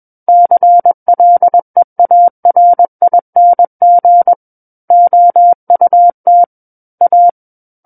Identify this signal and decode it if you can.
Morse code, heard as a keyed tone, about 21 wpm, 700 Hz. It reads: CLEARING OUT A
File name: Book One / 12-Chapter 12 - 21 WPM